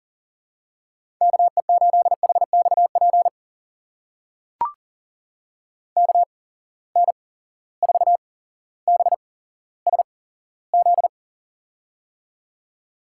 Morse code, heard as a keyed tone, about 38 wpm, spaced as for 17 wpm, 700 Hz. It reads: KE8HXP KN4BSZ
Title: USA Call Signs - 6 Characters - 8x Character Spacing - 40wpm